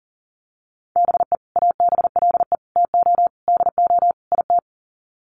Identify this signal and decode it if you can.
Morse code, heard as a keyed tone, about 40 wpm, 700 Hz. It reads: BE ABLE TO DO IT
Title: Sets of 5 Words Encore2 40wpm